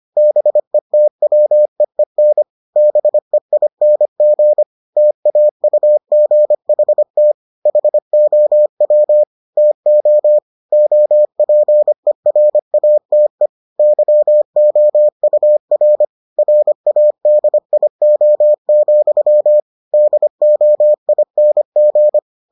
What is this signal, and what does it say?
Morse code, heard as a keyed tone, about 25 wpm, 600 Hz. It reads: BETWEEN BEING TAUGHT HOW TO OPERATE YOUR RADIO, DOING